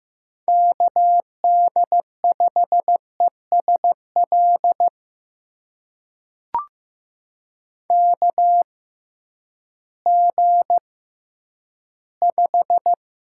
Morse code, heard as a keyed tone, about 15 wpm, spaced as for 10 wpm, 700 Hz. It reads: KD5ESL KG5
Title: USA Call Signs - 6x Character Spacing - 15wpm